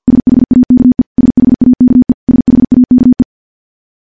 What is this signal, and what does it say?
Morse code, heard as a keyed tone, about 48 wpm, 250 Hz. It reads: SHAKE SHAKE SHAKE
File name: morse / morse-training-50WPM